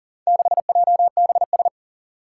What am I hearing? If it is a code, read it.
Morse code, heard as a keyed tone, about 39 wpm, 700 Hz. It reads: 6JBS
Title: Four Challenging Letter-Number Combinations 40wpm